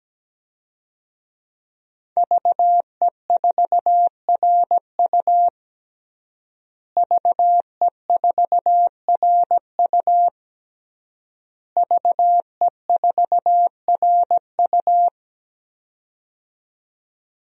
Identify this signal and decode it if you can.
Morse code, heard as a keyed tone, about 17 wpm, 700 Hz. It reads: VE4RU VE4RU VE4RU
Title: DXCC-100 Call Signs - 5 Characters - Repeated 3x - 17wpm